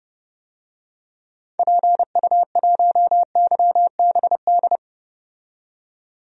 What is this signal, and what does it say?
Morse code, heard as a keyed tone, about 30 wpm, 700 Hz. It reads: PU1YBD
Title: DXCC-100 Call Signs - 6 Characters - 30wpm